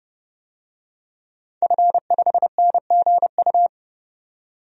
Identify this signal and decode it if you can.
Morse code, heard as a keyed tone, about 30 wpm, 700 Hz. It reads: F5NGU
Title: DXCC-100 Call Signs - 5 Characters - 30wpm